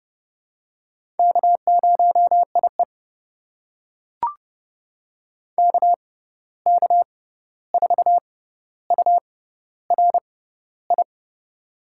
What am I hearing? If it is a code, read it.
Morse code, heard as a keyed tone, about 30 wpm, spaced as for 14 wpm, 700 Hz. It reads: K0IE KK4URI
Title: USA Call Signs - 6x Character Spacing - 30wpm